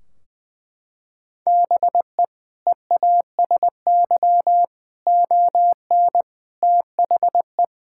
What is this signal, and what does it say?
Morse code, heard as a keyed tone, about 20 wpm, 700 Hz. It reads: BE EASY ON THE
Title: Sentences from Top 1600 Words 20wpm